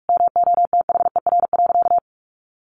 Morse code, heard as a keyed tone, about 44 wpm, 700 Hz. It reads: MOTHER.
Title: Sentences from Top 1300 Words 45wpm